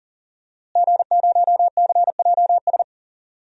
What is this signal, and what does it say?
Morse code, heard as a keyed tone, about 40 wpm, 700 Hz. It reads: G0CJS
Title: DXCC-100 Call Signs - 5 Characters - Repeated 2x - 40wpm